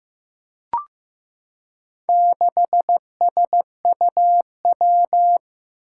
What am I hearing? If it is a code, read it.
Morse code, heard as a keyed tone, about 15 wpm, 700 Hz. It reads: 6SUW
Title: Four Challenging Letter-Number Combinations 15wpm